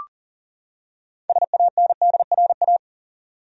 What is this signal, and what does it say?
Morse code, heard as a keyed tone, about 40 wpm, 700 Hz. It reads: SANDRA
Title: USA Common Names - 40wpm